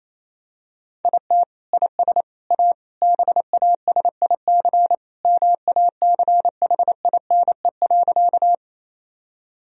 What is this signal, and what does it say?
Morse code, heard as a keyed tone, about 28 wpm, 700 Hz. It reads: IT IS A BASIC MACHINE.
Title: Sentences from Top 1000 Words - Review Part 3 - 28wpm